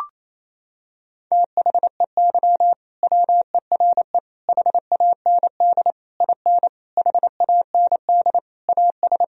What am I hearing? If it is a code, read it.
Morse code, heard as a keyed tone, about 28 wpm, 700 Hz. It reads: THEY WERE HAND IN HAND AS
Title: Sentences from Top 1000 Words - Review Part 2 - 28wpm